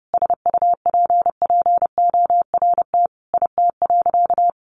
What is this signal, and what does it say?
Morse code, heard as a keyed tone, about 30 wpm, 700 Hz. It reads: SUPPORT IT.
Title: Sentences from Top 1300 Words 30wpm